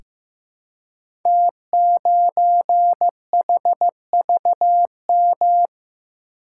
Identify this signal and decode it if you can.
Morse code, heard as a keyed tone, about 15 wpm, 700 Hz. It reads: T9HVM
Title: Code Groups - 5 Letter:Numbers - 15wpm